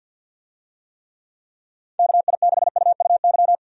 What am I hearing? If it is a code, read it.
Morse code, heard as a keyed tone, about 51 wpm, 700 Hz. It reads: KI6UUY
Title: USA Call Signs - 6 Characters - 50wpm